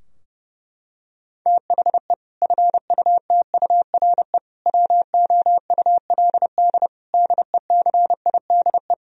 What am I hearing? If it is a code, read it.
Morse code, heard as a keyed tone, about 30 wpm, 700 Hz. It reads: THE FUTURE WOULD DECIDE
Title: Sentences from Top 1000 Words - Review Part 3 - 30wpm